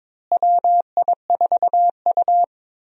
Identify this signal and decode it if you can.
Morse code, heard as a keyed tone, about 22 wpm, 700 Hz. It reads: WI4U
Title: USA Call Signs - 4 Characters - Repeated 3x - 22wpm